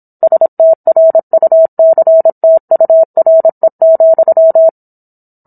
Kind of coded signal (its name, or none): Morse code